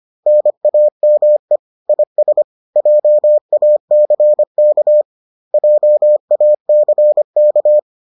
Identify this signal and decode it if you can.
Morse code, heard as a keyed tone, about 25 wpm, 600 Hz. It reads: NAME IS JACK JACK